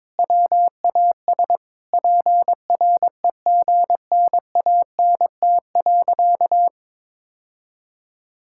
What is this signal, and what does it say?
Morse code, heard as a keyed tone, about 22 wpm, 700 Hz. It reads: WAS PREGNANT.
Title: Sentences from Top 1000 Words - Review Part 6 - 22wpm